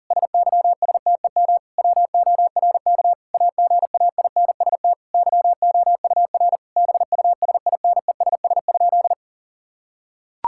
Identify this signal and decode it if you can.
Morse code, heard as a keyed tone, about 40 wpm, 700 Hz. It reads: SYSTEM WORK AGAINST YOUR BUSINESS?
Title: Sentences from Top 1000 Words - Review Part 5 - 40wpm